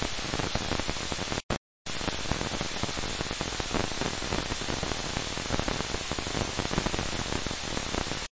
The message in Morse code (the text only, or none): none